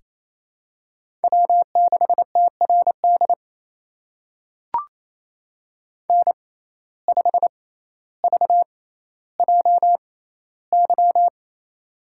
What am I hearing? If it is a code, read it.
Morse code, heard as a keyed tone, about 28 wpm, spaced as for 14 wpm, 700 Hz. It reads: W6TRD N5VJY
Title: USA Call Signs - 5 Characters - 6x Character Spacing - 28wpm